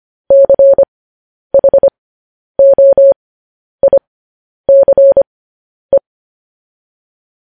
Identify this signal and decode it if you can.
Morse code, heard as a keyed tone, about 25 wpm, spaced as for 10 wpm, 550 Hz. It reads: CHOICE
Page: News Headlines in Morse Code at 10 WPM